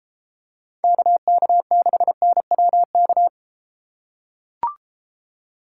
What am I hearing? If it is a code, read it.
Morse code, heard as a keyed tone, about 34 wpm, 700 Hz. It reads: KK6NWK